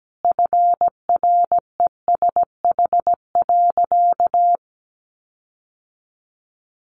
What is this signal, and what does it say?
Morse code, heard as a keyed tone, about 17 wpm, 700 Hz. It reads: FRESH.
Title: Sentences from Top 1500 Words 17wpm